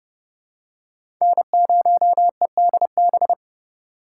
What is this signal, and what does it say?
Morse code, heard as a keyed tone, about 30 wpm, 700 Hz. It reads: N0EDB
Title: USA Call Signs - 2x Character Spacing - 30wpm